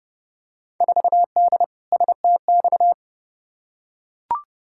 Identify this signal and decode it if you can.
Morse code, heard as a keyed tone, about 29 wpm, 700 Hz. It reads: 4D STX
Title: Field Day Exchanges Encore - 30wpm